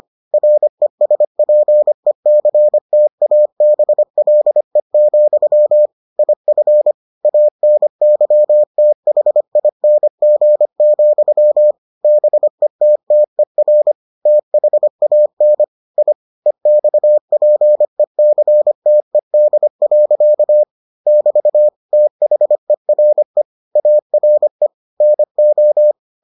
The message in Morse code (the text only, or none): RESPECTABLE, IF ANYTHING, BETTER THAN I EXPECTED. <BT> THERE ARE NO